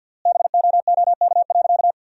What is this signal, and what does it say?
Morse code, heard as a keyed tone, about 50 wpm, 700 Hz. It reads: BOOK.
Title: 3x Word-Spacing - Sentences from Top 500 Words - 50wpm